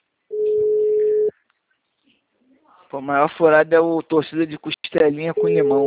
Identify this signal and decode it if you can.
Ringback tone (Europe, 425 Hz), with speech.